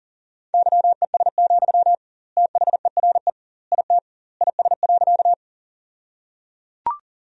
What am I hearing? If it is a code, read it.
Morse code, heard as a keyed tone, about 37 wpm, 700 Hz. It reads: YES, THERE IT IS.
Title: 2x Word-Spacing - Sentences from Top 500 Words - 40wpm